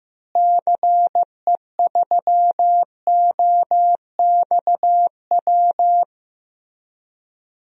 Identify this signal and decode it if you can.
Morse code, heard as a keyed tone, about 16 wpm, 700 Hz. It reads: CE3OXW
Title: DXCC-100 Call Signs - 6 Characters - Repeated 3x - 15wpm